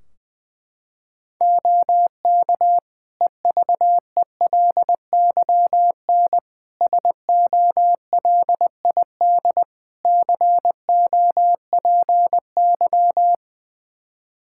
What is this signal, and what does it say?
Morse code, heard as a keyed tone, about 19 wpm, 700 Hz. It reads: OK EVELYN SOLID COPY
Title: Practice QSO - Exchange 3 - More Info - 20wpm